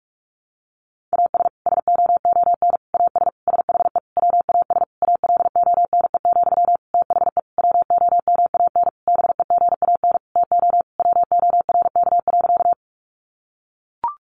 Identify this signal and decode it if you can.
Morse code, heard as a keyed tone, about 44 wpm, 700 Hz. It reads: AS SOON AS SHE WAS ALONE, THE WOMAN BEGAN TO WORK.
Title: Sentences from Top 1100 Words 45wpm